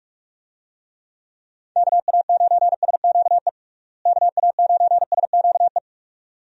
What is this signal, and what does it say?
Morse code, heard as a keyed tone, about 46 wpm, 700 Hz. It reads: KA9SQE KA9SQE
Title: USA Call Signs - 6 Characters - Repeated 2x - 45wpm